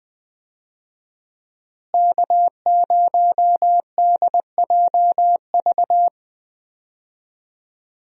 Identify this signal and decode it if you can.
Morse code, heard as a keyed tone, about 20 wpm, 700 Hz. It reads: K0DJV
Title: USA Call Signs - 5 Characters - Repeated 2x - 20wpm